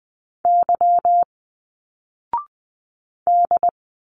Morse code, heard as a keyed tone, about 20 wpm, 700 Hz. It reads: Y D
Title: Single Letters 20wpm